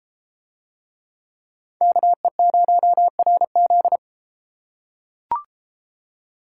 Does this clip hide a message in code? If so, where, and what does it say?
Yes: Morse code from 1.8 to 4.0 s, heard as a keyed tone, about 32 wpm, 700 Hz. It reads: KE0RZ